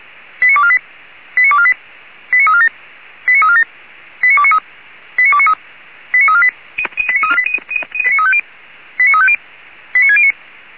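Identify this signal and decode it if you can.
Five-tone selective call (ZVEI), heard as quick runs of five tones, tones of about 70 ms. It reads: 89248 89248 89347 89347 89293 89293 89348 89350 89609
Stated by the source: ZVEI five-tone selective call 89406, received on SDR.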